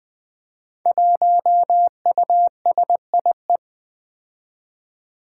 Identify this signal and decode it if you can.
Morse code, heard as a keyed tone, about 20 wpm, 700 Hz. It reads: 1USIE
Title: Five Challenging Letter-Number Combinations 20wpm